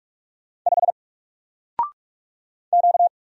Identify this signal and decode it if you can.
Morse code, heard as a keyed tone, about 44 wpm, 700 Hz. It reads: 5 Q
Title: Single Character (Letter:Number:Punctuation:Prosign) Alt 45wpm